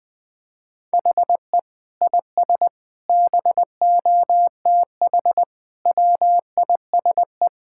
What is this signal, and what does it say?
Morse code, heard as a keyed tone, about 20 wpm, 700 Hz. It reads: HE IS BOTH WISE
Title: Sentences from Top 1600 Words 20wpm